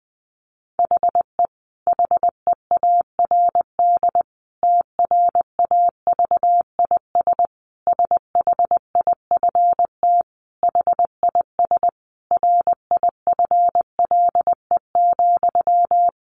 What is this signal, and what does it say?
Morse code, heard as a keyed tone, about 19 wpm, 700 Hz. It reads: HE HEARD TRAVIS SHIFT HIS RIFLE,
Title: A Sound Of Thunder 20wpm